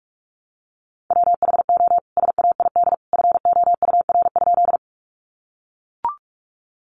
Morse code, heard as a keyed tone, about 44 wpm, 700 Hz. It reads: WHO SAID FOUR?
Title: Sentences from Top 300 Words 45wpm